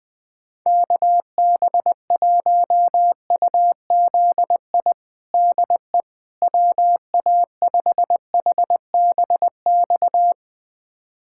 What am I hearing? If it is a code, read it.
Morse code, heard as a keyed tone, about 20 wpm, 700 Hz. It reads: KB1UZI DE WA5HBX
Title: USA Callsign DE Callsign 20wpm